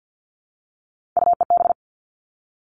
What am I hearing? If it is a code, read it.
Morse code, heard as a keyed tone, about 50 wpm, 700 Hz. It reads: UEB
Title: All Cut Numbers - 3-Digit 50wpm